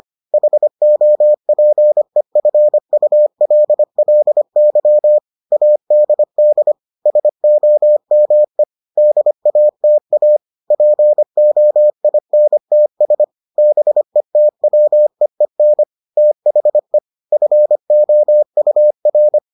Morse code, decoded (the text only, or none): HOPEFULLY ADD SOME DATA POINTS BETWEEN THE FOUR